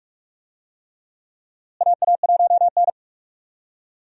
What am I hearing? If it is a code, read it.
Morse code, heard as a keyed tone, about 46 wpm, 700 Hz. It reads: AA1N
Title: USA Call Signs - 4 Characters - Repeated 3x - 45wpm